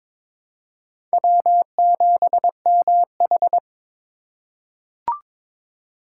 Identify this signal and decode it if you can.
Morse code, heard as a keyed tone, about 22 wpm, 700 Hz. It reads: W7MH
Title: USA Call Signs - 4 Characters - 2x Character Spacing - 22wpm